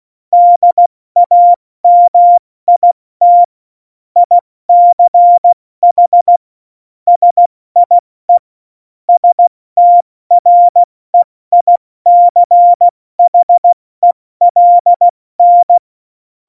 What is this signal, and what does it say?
Morse code, heard as a keyed tone, about 16 wpm, 700 Hz. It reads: DAMIT ICH SIE STREICHELN
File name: Morsekurs / Witze / Witz 14. November 2017, 70 F80